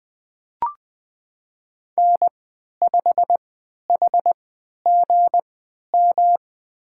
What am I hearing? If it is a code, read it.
Morse code, heard as a keyed tone, about 20 wpm, spaced as for 11 wpm, 700 Hz. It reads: N5HGM
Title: USA Call Signs - 5 Characters - 3x Character Spacing - 20wpm